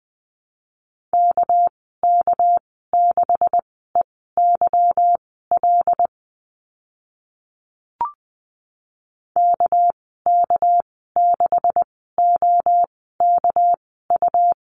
Morse code, heard as a keyed tone, about 20 wpm, spaced as for 14 wpm, 700 Hz. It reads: KK6EYL KK6OKU
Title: 245 Lesson 39g - Letter z - Call Signs 20wpm